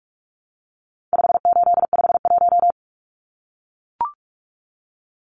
Morse code, heard as a keyed tone, about 44 wpm, 700 Hz. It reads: 5851
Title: Numbers - 4-Digit 45wpm